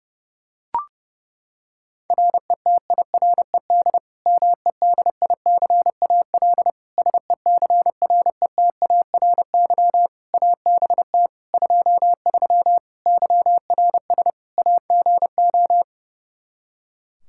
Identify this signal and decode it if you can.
Morse code, heard as a keyed tone, about 30 wpm, 700 Hz. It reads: RETIRED MEDICAL SECRETARY ABT 23 YRS AGO
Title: K1USN SST QSO Practice 30wpm